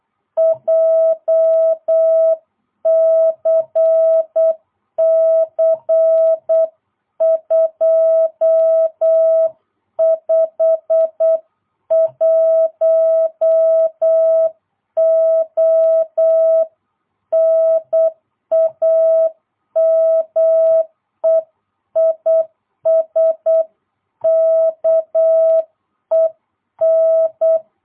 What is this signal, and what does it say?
Morse code, heard as a keyed tone, about 8 wpm, 660 Hz. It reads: JCC 251O NAME IS KEN